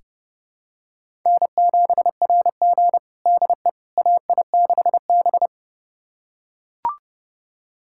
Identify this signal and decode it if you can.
Morse code, heard as a keyed tone, about 30 wpm, 700 Hz. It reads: N7RG DE AI6B
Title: USA Callsign DE Callsign - 4 Characters - 30wpm